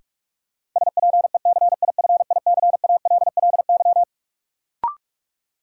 Morse code, heard as a keyed tone, about 42 wpm, 700 Hz. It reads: SPECIFICALLY
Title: Sentences from Top 1100 Words - New Word Drill - 45wpm